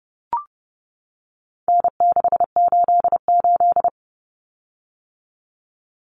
Morse code, heard as a keyed tone, about 30 wpm, 700 Hz. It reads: N688
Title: Common Cut Numbers - 4-Digit 30wpm